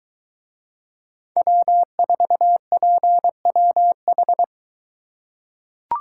Morse code, heard as a keyed tone, about 22 wpm, 700 Hz. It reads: W4PWH